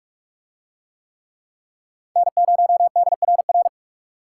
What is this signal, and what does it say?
Morse code, heard as a keyed tone, about 38 wpm, 700 Hz. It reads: N0DRR